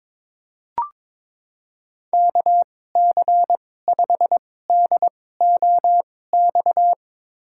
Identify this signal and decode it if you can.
Morse code, heard as a keyed tone, about 22 wpm, spaced as for 16 wpm, 700 Hz. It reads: KC5DOX
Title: USA Call Signs - 2x Character Spacing - 22wpm